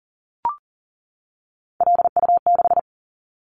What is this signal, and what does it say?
Morse code, heard as a keyed tone, about 42 wpm, 690 Hz. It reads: LU6
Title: Callsign Prefixes for Top 100 DXCC - 40wpm